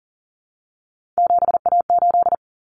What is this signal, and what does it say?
Morse code, heard as a keyed tone, about 40 wpm, 700 Hz. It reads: 7A8
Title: Common Cut Numbers - 3-Digit 40wpm